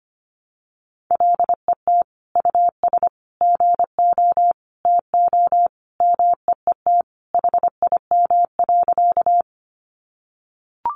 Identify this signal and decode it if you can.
Morse code, heard as a keyed tone, about 25 wpm, 700 Hz. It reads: LET US GO TO MEET HIM.
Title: Sentences from Top 1100 Words 25wpm